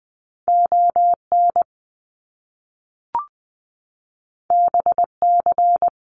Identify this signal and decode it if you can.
Morse code, heard as a keyed tone, about 20 wpm, 700 Hz. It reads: ON BC
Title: Canadian Provinces and Territories - Repeated 3x - 20wpm